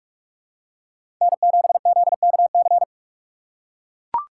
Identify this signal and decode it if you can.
Morse code, heard as a keyed tone, about 43 wpm, 700 Hz. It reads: N7ZKC
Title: USA Call Signs - 4x Character Spacing - 45wpm